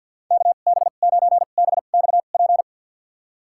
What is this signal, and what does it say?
Morse code, heard as a keyed tone, about 52 wpm, spaced as for 36 wpm, 700 Hz. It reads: KB9BXP
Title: USA Call Signs - 2x Character Spacing - 50wpm